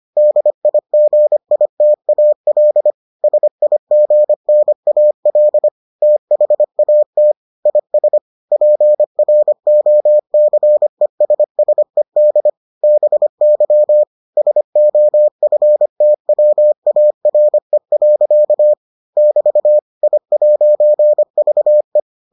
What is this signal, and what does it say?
Morse code, heard as a keyed tone, about 25 wpm, 600 Hz. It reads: DIGITAL SIGNAL THAT IS PROCESSED BY SOFTWARE. <BT> I'VE